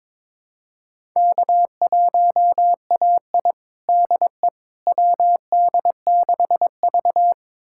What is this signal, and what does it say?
Morse code, heard as a keyed tone, about 22 wpm, 700 Hz. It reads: K1AI DE WD6V
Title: USA Callsign DE Callsign - 4 Characters - 22wpm